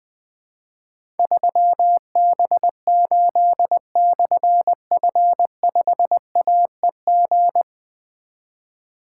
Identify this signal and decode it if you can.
Morse code, heard as a keyed tone, about 20 wpm, 700 Hz. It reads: 3B8/F5AEG
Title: Difficult Call Signs 20wpm